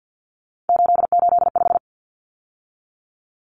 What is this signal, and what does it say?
Morse code, heard as a keyed tone, about 50 wpm, 700 Hz. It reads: 885
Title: Numbers - 3-Digit 50wpm